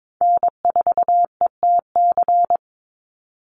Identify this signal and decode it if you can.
Morse code, heard as a keyed tone, about 22 wpm, 700 Hz. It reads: N4ETC
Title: Call Signs - 22wpm